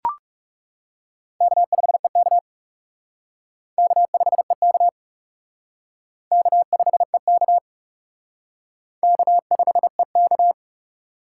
Morse code, heard as a keyed tone, about 40 wpm, 700 Hz. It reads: K5EK K5EK K5EK K5EK